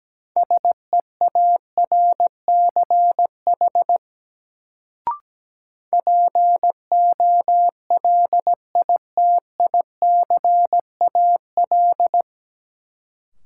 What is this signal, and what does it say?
Morse code, heard as a keyed tone, about 17 wpm, 700 Hz. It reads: SEARCH POLITICAL
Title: Sentences from Top 800 Words - New Word Drill - 17wpm